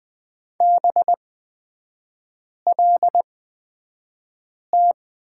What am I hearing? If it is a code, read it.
Morse code, heard as a keyed tone, about 20 wpm, spaced as for 5 wpm, 700 Hz. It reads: BLT
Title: 243 Lesson 39e - Letter z - Three Characters 20wpm